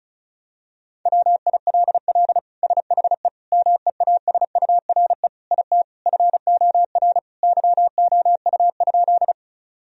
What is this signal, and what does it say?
Morse code, heard as a keyed tone, about 36 wpm, 700 Hz. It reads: WILL SHE MEASURE IT FOR YOU?
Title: Sentences from Top 1000 Words - Review Part 2 - 35wpm